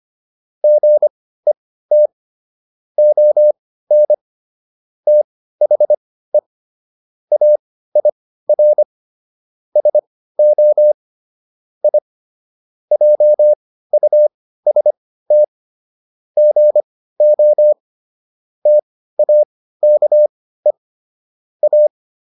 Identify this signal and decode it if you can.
Morse code, heard as a keyed tone, about 25 wpm, spaced as for 15 wpm, 600 Hz. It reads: GET ON THE AIR SO I JUST GO TAKE A